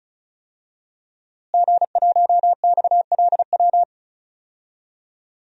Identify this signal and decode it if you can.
Morse code, heard as a keyed tone, about 36 wpm, 700 Hz. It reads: G1XLW
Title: DXCC-100 Call Signs - 5 Characters - Repeated 2x - 35wpm